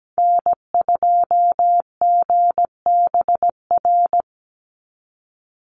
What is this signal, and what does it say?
Morse code, heard as a keyed tone, about 17 wpm, 700 Hz. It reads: N2GBR
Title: NA SOTA Activator Callsigns 17wpm